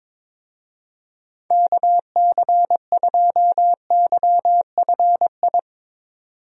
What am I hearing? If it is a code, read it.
Morse code, heard as a keyed tone, about 21 wpm, 700 Hz. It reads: KC2YFI
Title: USA Call Signs - 6 Characters - 22wpm